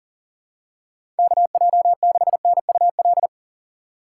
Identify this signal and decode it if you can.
Morse code, heard as a keyed tone, about 40 wpm, 700 Hz. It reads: KJ6NUL
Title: USA Call Signs - 6 Characters - 3x Character Spacing - 40wpm